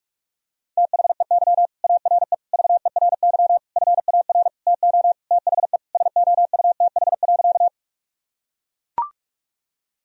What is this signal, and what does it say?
Morse code, heard as a keyed tone, about 43 wpm, 700 Hz. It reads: THEY ARE VERY FAR TO THE SOUTH.
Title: Sentences from Top 1000 Words - Review Part 2 - 45wpm